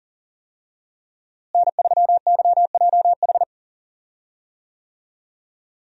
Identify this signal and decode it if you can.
Morse code, heard as a keyed tone, about 37 wpm, 700 Hz. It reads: N3YJH
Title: USA Call Signs - 5 Characters - Repeated 3x - 40wpm